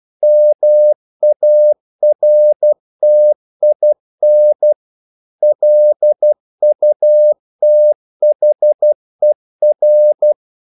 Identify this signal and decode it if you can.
Morse code, heard as a keyed tone, about 12 wpm, 600 Hz. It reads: MARTIN LUTHER
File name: CW / Practice / EnglishQuotations0112